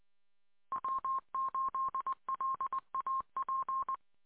Morse code, heard as a keyed tone, about 20 wpm, 1060 Hz. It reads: W8LAP